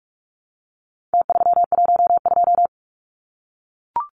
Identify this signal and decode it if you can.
Morse code, heard as a keyed tone, about 44 wpm, 700 Hz. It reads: T312
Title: Numbers - Contest Style - 4-Digit 45wpm